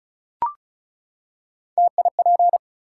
Morse code, heard as a keyed tone, about 34 wpm, 700 Hz. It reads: TIP
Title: Three Letter Words - 35wpm